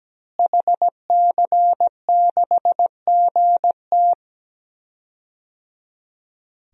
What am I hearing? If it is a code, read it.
Morse code, heard as a keyed tone, about 17 wpm, 700 Hz. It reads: HC6GT
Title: DXCC-100 Call Signs - 5 Characters - Repeated 3x - 17wpm